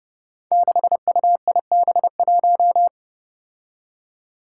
Morse code, heard as a keyed tone, about 30 wpm, 700 Hz. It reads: 6UIB1
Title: Five Challenging Letter-Number Combinations 30wpm